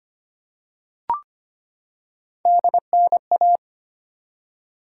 Morse code, heard as a keyed tone, about 25 wpm, 700 Hz. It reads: DNA
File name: Three Letter-Number Combinations - 25wpm